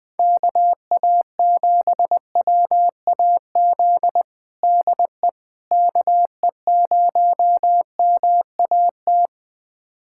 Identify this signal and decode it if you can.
Morse code, heard as a keyed tone, about 21 wpm, 700 Hz. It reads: KA7WAZ DE KE0MAT
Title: USA Callsign DE Callsign - 6 Characters - 20wpm